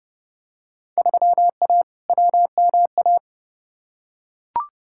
Morse code, heard as a keyed tone, about 32 wpm, 700 Hz. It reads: 3A WMA
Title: Field Day Exchanges - 30wpm